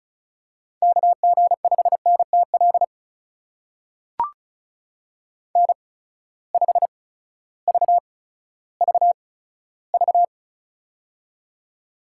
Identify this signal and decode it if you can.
Morse code, heard as a keyed tone, about 36 wpm, spaced as for 16 wpm, 700 Hz. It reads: KG5NTL N5VVV
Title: USA Call Signs - 8x Character Spacing - 35wpm